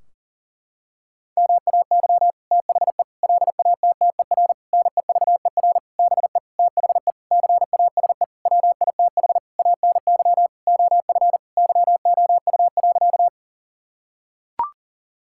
Morse code, heard as a keyed tone, about 40 wpm, 700 Hz. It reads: MAY THE LATTER NEVER BE THE CASE WITH ANY OF YOU.
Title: Sentences from Top 1600 Words 40wpm